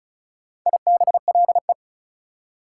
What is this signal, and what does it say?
Morse code, heard as a keyed tone, about 46 wpm, 690 Hz. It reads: IBLE